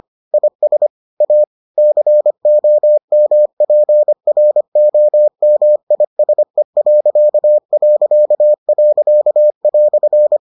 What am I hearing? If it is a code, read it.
Morse code, heard as a keyed tone, about 25 wpm, 600 Hz. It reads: IS A COMPROMISE..."